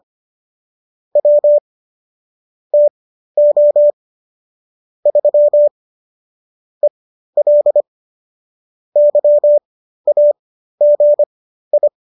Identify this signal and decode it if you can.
Morse code, heard as a keyed tone, about 25 wpm, spaced as for 13 wpm, 590 Hz. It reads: W TO 3 EL YAGI